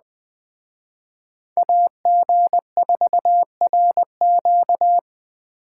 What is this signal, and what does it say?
Morse code, heard as a keyed tone, about 20 wpm, 700 Hz. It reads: AG4RQ
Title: Speed Racing - USA 2x2 Call Signs 17wpm